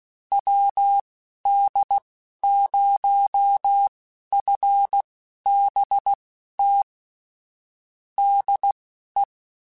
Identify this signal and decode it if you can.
Morse code, heard as a keyed tone, about 16 wpm, spaced as for 12 wpm, 800 Hz. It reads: WD0FBT DE
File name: (spaced) / QSOs Spaced-mp3s / WD0FBT 1 @15 sp